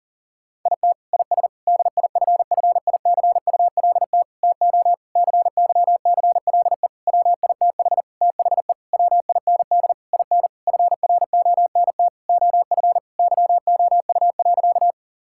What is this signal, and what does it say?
Morse code, heard as a keyed tone, about 40 wpm, 700 Hz. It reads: IT IS DIFFICULT TO CYCLE WITH THE WIND IN FRONT OF YOU.
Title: Sentences from Top 1000 Words - Review Part 6 - 40wpm